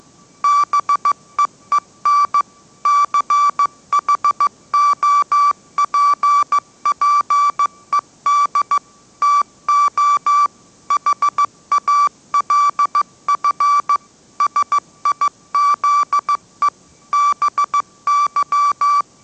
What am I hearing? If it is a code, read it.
Morse code, heard as a keyed tone, about 15 wpm, 1160 Hz. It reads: BEEN CHOPPED TO HALF SIZE BY
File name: mrx / mrx066 16wpm